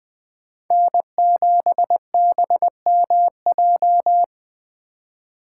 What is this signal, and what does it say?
Morse code, heard as a keyed tone, about 21 wpm, 700 Hz. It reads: N7BMJ
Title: USA Call Signs - 5 Characters - 4x Character Spacing - 20wpm